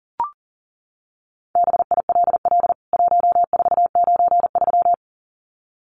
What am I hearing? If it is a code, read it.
Morse code, heard as a keyed tone, about 40 wpm, 700 Hz. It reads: BILL 1493
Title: CWT Exchange 40wpm